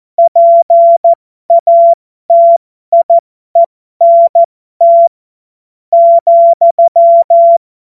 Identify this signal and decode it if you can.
Morse code, heard as a keyed tone, about 14 wpm, spaced as for 11 wpm, 680 Hz. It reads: PATIENT,
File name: Morsekurs / Witze